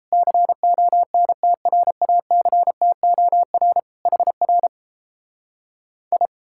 Morse code, heard as a keyed tone, about 38 wpm, 700 Hz. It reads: CONTRACTOR HR I